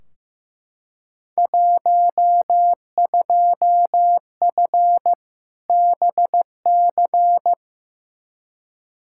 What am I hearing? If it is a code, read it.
Morse code, heard as a keyed tone, about 15 wpm, 700 Hz. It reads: 12F BC
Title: Field Day Exchanges Encore - 15wpm